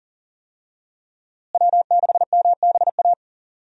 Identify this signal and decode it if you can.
Morse code, heard as a keyed tone, about 40 wpm, 700 Hz. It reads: W6MBA
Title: USA Call Signs - 5 Characters - Repeated 2x - 40wpm